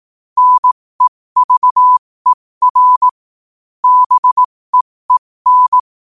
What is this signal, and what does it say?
Morse code, heard as a keyed tone, about 18 wpm, spaced as for 15 wpm, 1000 Hz. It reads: NEVER BEEN
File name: qotdfiles / 15wpm